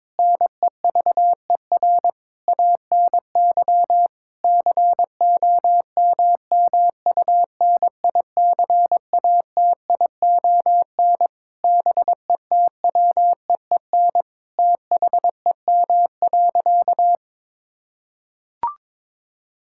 Morse code, heard as a keyed tone, about 22 wpm, 700 Hz. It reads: NEVER ANY COMMUNICATION BETWEEN THEM.
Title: Sentences from Top 1000 Words - Review Part 6 - 22wpm